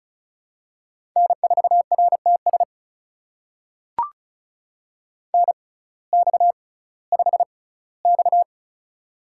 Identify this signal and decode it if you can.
Morse code, heard as a keyed tone, about 34 wpm, spaced as for 19 wpm, 700 Hz. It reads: N4RTS NX5X
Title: USA Call Signs - 6x Character Spacing - 35wpm